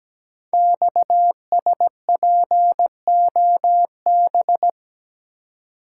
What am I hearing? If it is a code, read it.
Morse code, heard as a keyed tone, about 17 wpm, 700 Hz. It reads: XSPOB
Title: Code Groups - 5 Letters - 17wpm